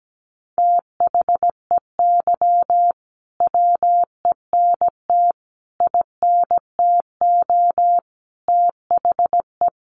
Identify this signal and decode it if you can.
Morse code, heard as a keyed tone, about 17 wpm, 700 Hz. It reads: THEY WENT INTO THE
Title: Sentences from Top 900 Words 17wpm